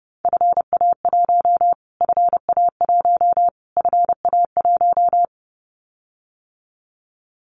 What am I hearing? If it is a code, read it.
Morse code, heard as a keyed tone, about 30 wpm, 700 Hz. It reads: FA1 FA1 FA1
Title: Callsign Prefixes for Top 25 DXCC - Repeated 3x - 30wpm